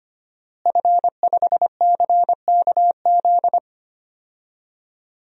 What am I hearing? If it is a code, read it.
Morse code, heard as a keyed tone, about 25 wpm, 700 Hz. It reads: F5CKZ
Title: DXCC-100 Call Signs - 5 Characters - Repeated 2x - 25wpm